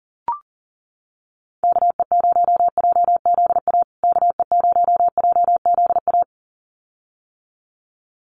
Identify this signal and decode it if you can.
Morse code, heard as a keyed tone, about 40 wpm, 700 Hz. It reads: KE0JZA KE0JZA
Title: USA Call Signs Repeated 2x 40wpm